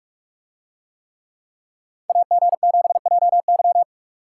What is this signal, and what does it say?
Morse code, heard as a keyed tone, about 46 wpm, 700 Hz. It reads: AG7JY